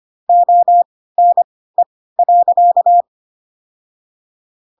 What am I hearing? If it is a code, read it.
Morse code, heard as a keyed tone, about 25 wpm, spaced as for 16 wpm, 700 Hz. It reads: ONE.
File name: Pinocchio / Pinocchio ch7 10 25 ik1hge